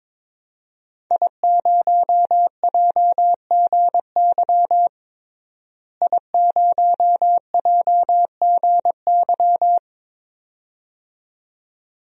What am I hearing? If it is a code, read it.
Morse code, heard as a keyed tone, about 21 wpm, 700 Hz. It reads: I0JGY I0JGY
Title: DXCC-100 Call Signs - 5 Characters - Repeated 2x - 22wpm